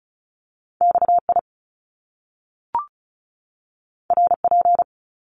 Morse code, heard as a keyed tone, about 36 wpm, 700 Hz. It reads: XI RP
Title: Top Two Letter Combinations in Words 35wpm